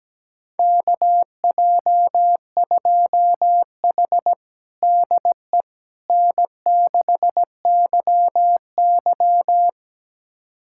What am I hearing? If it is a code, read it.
Morse code, heard as a keyed tone, about 17 wpm, 700 Hz. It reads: KJ2H DE N6YY
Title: USA Callsign DE Callsign - 4 Characters - 17wpm